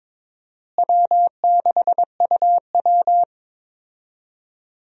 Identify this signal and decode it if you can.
Morse code, heard as a keyed tone, about 23 wpm, 700 Hz. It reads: W6UW
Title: USA Call Signs - 4 Characters - Repeated 2x - 22wpm